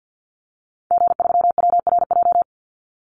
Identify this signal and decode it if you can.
Morse code, heard as a keyed tone, about 50 wpm, 700 Hz. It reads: G3WRJ
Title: Call Signs Repeated 3x - 50wpm